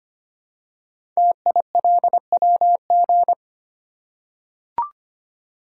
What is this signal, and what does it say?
Morse code, heard as a keyed tone, about 25 wpm, 700 Hz. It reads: TILWG
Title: Code Groups - 5 Letters - 25wpm